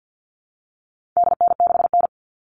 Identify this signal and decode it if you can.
Morse code, heard as a keyed tone, about 48 wpm, 700 Hz. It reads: DN6N
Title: All Cut Numbers - 4-Digit 50wpm